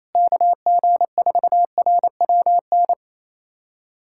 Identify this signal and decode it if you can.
Morse code, heard as a keyed tone, about 28 wpm, 700 Hz. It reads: KG4RWN
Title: USA Callsign DE Callsign - 6 Characters - 28wpm